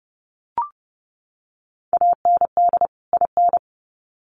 Morse code, heard as a keyed tone, about 30 wpm, 700 Hz. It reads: AND IN
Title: Sets of 2 Words 30wpm